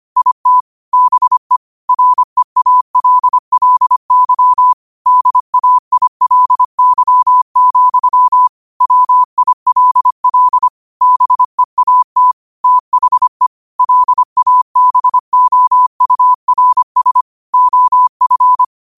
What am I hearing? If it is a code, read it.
Morse code, heard as a keyed tone, about 25 wpm, 1000 Hz. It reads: IT BE REALLY DAILY, WILL BEAT THE LABOURS OF
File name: qotdfiles / 25wpm